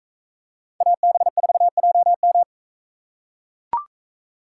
Morse code, heard as a keyed tone, about 41 wpm, 700 Hz. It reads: AB4JM